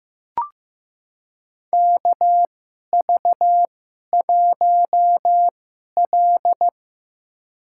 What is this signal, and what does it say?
Morse code, heard as a keyed tone, about 15 wpm, spaced as for 11 wpm, 700 Hz. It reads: KV1L
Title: USA Call Signs - 4 Characters - 2x Character Spacing - 15wpm